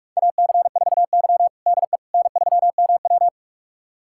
Morse code, heard as a keyed tone, about 41 wpm, 700 Hz. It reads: AC4Y DE N3MW
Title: USA Callsign DE Callsign - 4 Characters - 45wpm